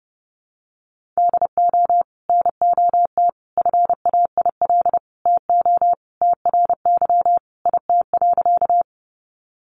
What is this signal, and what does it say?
Morse code, heard as a keyed tone, about 30 wpm, 700 Hz. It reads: DO NOT FAIL TO TRY IT.
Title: Sentences from Top 1100 Words 30wpm